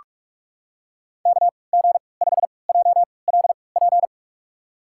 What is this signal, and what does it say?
Morse code, heard as a keyed tone, about 44 wpm, spaced as for 25 wpm, 700 Hz. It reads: KG5JLP
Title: USA Call Signs - 3x Character Spacing - 45wpm